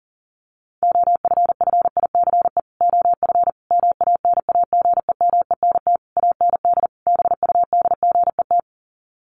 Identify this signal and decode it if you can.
Morse code, heard as a keyed tone, about 40 wpm, 700 Hz. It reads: OFFICE OF MANAGEMENT AND BUDGET
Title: Sets of 5 Words Encore2 40wpm